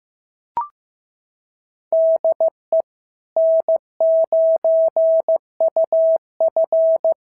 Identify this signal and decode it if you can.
Morse code, heard as a keyed tone, about 15 wpm, 650 Hz. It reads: DE N9UF
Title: SOTA QSO Practice 15wpm